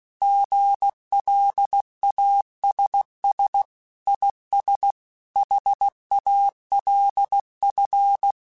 Morse code, heard as a keyed tone, about 16 wpm, 800 Hz. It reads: GLASS IS HALF